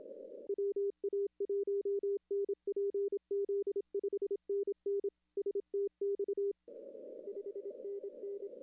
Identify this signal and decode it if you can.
Morse code, heard as a keyed tone, about 27 wpm, 410 Hz. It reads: WA1NPZ 5NN STX 5NN
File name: contests / arrl-160-k5tr-op-w5tn -0300